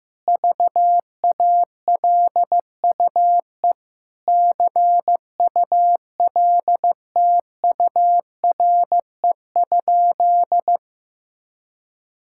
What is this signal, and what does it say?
Morse code, heard as a keyed tone, about 16 wpm, 700 Hz. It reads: VALUE CULTURE?
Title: Sentences from Top 1000 Words - Review Part 6 - 15wpm